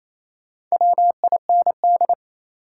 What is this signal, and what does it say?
Morse code, heard as a keyed tone, about 28 wpm, 700 Hz. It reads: WIND
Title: Sentences from Top 400 Words - New Word Drill - 28wpm